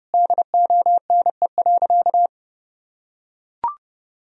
Morse code, heard as a keyed tone, about 30 wpm, 700 Hz. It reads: DONE.
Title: Sentences from Top 1000 Words - Review Part 1 - 30wpm